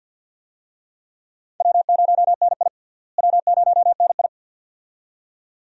Morse code, heard as a keyed tone, about 53 wpm, 700 Hz. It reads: W0NI W0NI
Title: USA Call Signs - 4 Characters - Repeated 2x - 50wpm